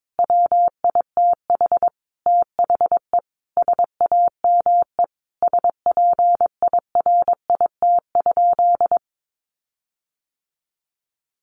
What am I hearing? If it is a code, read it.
Morse code, heard as a keyed tone, about 22 wpm, 700 Hz. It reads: WITH THE SAME SPIRIT?
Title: Sentences from Top 1400 Words 22wpm